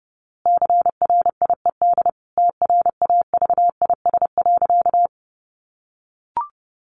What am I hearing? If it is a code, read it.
Morse code, heard as a keyed tone, about 30 wpm, 700 Hz. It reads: CRIED TRAVIS.
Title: A Sound Of Thunder 30wpm